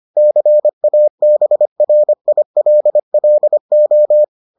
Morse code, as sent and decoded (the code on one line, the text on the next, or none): -.-. .- -... .-. .. .-.. .-.. ---
CABRILLO